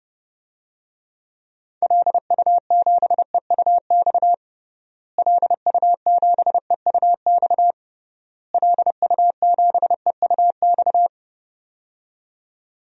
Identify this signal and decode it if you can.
Morse code, heard as a keyed tone, about 30 wpm, 700 Hz. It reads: LU7EUX LU7EUX LU7EUX
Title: DXCC-100 Call Signs - 6 Characters - Repeated 3x - 30wpm